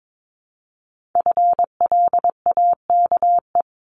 Morse code, heard as a keyed tone, about 22 wpm, 700 Hz. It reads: FLAKE
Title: Multi-letter Phonemes 22wpm